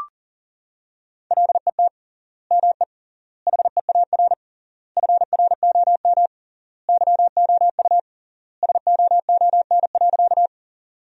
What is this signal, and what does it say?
Morse code, heard as a keyed tone, about 37 wpm, 700 Hz. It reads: LET ME HEAR FROM YOU SOON.
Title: 3x Word-Spacing - Sentences from Top 400 Words - 40wpm